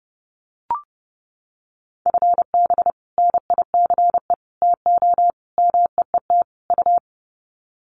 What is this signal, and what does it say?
Morse code, heard as a keyed tone, about 30 wpm, 700 Hz. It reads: FB NICE TO MEET U
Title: Common QSO Phrases 30wpm